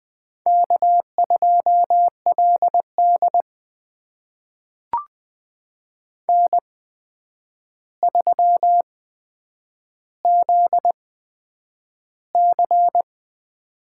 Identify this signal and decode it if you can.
Morse code, heard as a keyed tone, about 20 wpm, spaced as for 9 wpm, 700 Hz. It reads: K2LD N3ZC
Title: USA Call Signs - 4 Characters - 8x Character Spacing - 20wpm